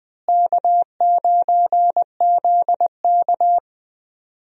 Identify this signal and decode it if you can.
Morse code, heard as a keyed tone, about 20 wpm, 700 Hz. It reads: K9ZK
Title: USA Call Signs - 4 Characters - Repeated 2x - 20wpm